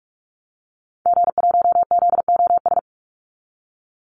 Morse code, heard as a keyed tone, about 44 wpm, 700 Hz. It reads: G1ZOS
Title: Call Signs Repeated 2x - 45wpm